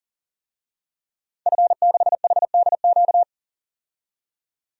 Morse code, heard as a keyed tone, about 40 wpm, 700 Hz. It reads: F6HDQ
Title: DXCC-100 Call Signs - 5 Characters - 40wpm